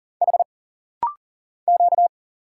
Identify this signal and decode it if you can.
Morse code, heard as a keyed tone, about 40 wpm, 700 Hz. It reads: H Q
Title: Single Letters - Rapid-Fire 40wpm